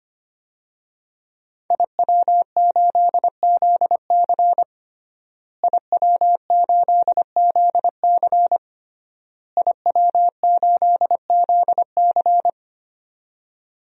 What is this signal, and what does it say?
Morse code, heard as a keyed tone, about 25 wpm, 700 Hz. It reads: IW8ZC IW8ZC IW8ZC
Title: DXCC-100 Call Signs - 5 Characters - Repeated 3x - 25wpm